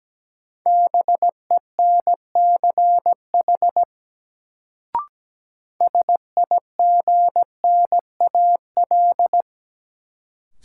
Morse code, heard as a keyed tone, about 17 wpm, 700 Hz. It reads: BENCH SIGNAL
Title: Sentences from Top 1400 Words - New Word Drill Alt - 17wpm